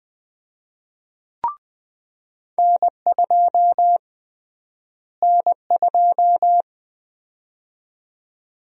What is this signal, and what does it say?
Morse code, heard as a keyed tone, about 20 wpm, 700 Hz. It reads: N2 N2
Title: USA Call Sign Prefixes - Repeated 2x - 20wpm